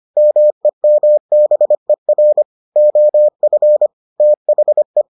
Morse code, heard as a keyed tone, about 25 wpm, 600 Hz. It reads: MEMBER OF THE